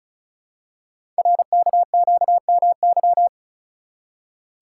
Morse code, heard as a keyed tone, about 36 wpm, 700 Hz. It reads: RKQMY
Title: Code Groups - 5 Letter:Numbers - 35wpm